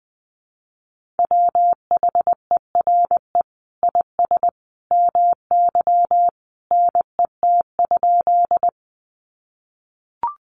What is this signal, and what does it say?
Morse code, heard as a keyed tone, about 20 wpm, 700 Hz. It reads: WHERE IS MY NET?
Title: Sentences from Top 1400 Words 20wpm